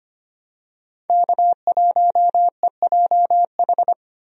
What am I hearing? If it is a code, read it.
Morse code, heard as a keyed tone, about 25 wpm, 700 Hz. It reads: K1EJH
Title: USA Call Signs - 5 Characters - 25wpm